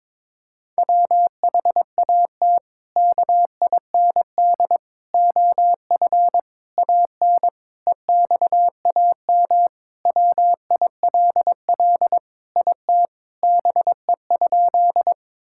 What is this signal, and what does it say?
Morse code, heard as a keyed tone, about 22 wpm, 700 Hz. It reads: WHAT KIND OF AN EXAM WILL IT BE?
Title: Sentences from Top 1000 Words - Review Part 5 - 22wpm